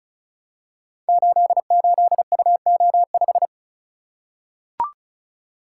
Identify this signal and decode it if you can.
Morse code, heard as a keyed tone, about 36 wpm, 700 Hz. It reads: 88UO5
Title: Code Groups - 5 Letter:Numbers - 35wpm